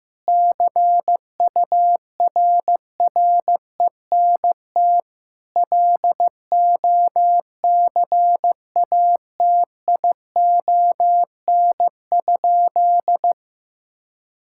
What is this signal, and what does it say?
Morse code, heard as a keyed tone, about 15 wpm, 700 Hz. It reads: CURRENT LOCATION?
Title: Sentences from Top 1000 Words - Review Part 6 - 15wpm